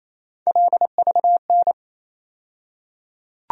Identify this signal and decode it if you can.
Morse code, heard as a keyed tone, about 28 wpm, 700 Hz. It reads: LVN
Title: USA Call Sign Postfixes - 3 Letters 28wpm